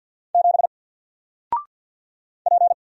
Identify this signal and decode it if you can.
Morse code, heard as a keyed tone, about 48 wpm, 700 Hz. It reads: 7 P
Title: Single Character (Letter:Number:Punctuation:Prosign) 50wpm